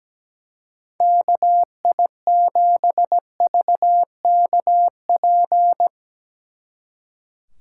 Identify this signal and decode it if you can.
Morse code, heard as a keyed tone, about 18 wpm, 700 Hz. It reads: KI7VKP
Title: Speed Racing - USA 2x3 Call Signs 17wpm